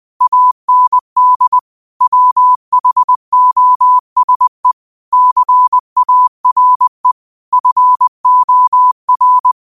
Morse code, heard as a keyed tone, about 20 wpm, 1000 Hz. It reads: AND WHOSE CARE FOR
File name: qotdfiles / 20wpm